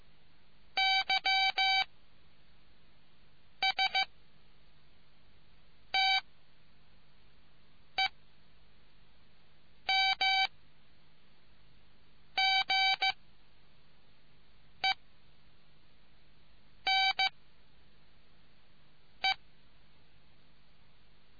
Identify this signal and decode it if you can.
Morse code, heard as a keyed tone, about 15 wpm, spaced as for 4 wpm, 770 Hz. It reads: YSTEMGENE